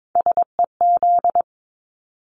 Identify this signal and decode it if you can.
Morse code, heard as a keyed tone, about 22 wpm, 700 Hz. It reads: SEZ
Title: Abbreviations Repeated 3x - 22wpm